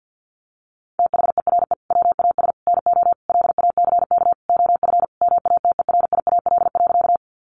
Morse code, heard as a keyed tone, about 50 wpm, 700 Hz. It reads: THERE WAS NO LACK OF MATERIAL.
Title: Sentences from Top 1100 Words 50wpm